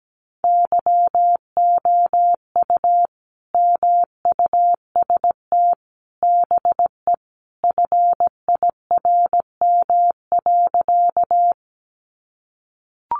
Morse code, heard as a keyed tone, about 17 wpm, 700 Hz. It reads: YOU MUST BE FIRM.
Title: Sentences from Top 1400 Words 17wpm